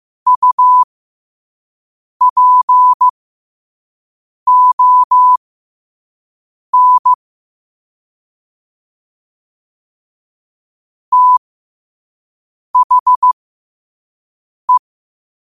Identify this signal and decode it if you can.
Morse code, heard as a keyed tone, about 15 wpm, spaced as for 5 wpm, 1000 Hz. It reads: UPON THE